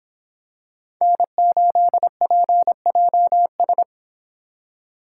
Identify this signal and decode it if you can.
Morse code, heard as a keyed tone, about 26 wpm, 700 Hz. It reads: N8PJS